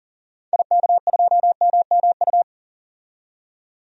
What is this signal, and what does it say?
Morse code, heard as a keyed tone, about 40 wpm, 700 Hz. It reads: IK2MMU
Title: DXCC-100 Call Signs - 6 Characters - 40wpm